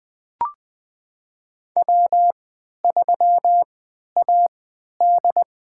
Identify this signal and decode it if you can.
Morse code, heard as a keyed tone, about 21 wpm, spaced as for 11 wpm, 700 Hz. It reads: W3AD
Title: USA Call Signs - 4 Characters - 3x Character Spacing - 20wpm